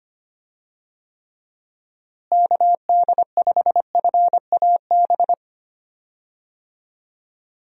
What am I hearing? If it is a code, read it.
Morse code, heard as a keyed tone, about 25 wpm, 700 Hz. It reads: KD5FAB
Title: USA Call Signs - 6 Characters - Repeated 2x - 25wpm